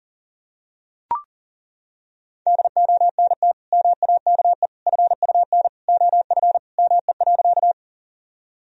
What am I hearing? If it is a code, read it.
Morse code, heard as a keyed tone, about 40 wpm, 700 Hz. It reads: DONT MAKE FUN OF ME.
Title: Sentences from Top 1000 Words - Review Part 4 - 40wpm